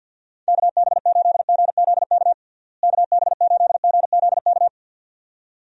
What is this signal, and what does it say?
Morse code, heard as a keyed tone, about 52 wpm, 700 Hz. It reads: KB8GZK KB8GZK
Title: USA Call Signs - 6 Characters - Repeated 2x - 50wpm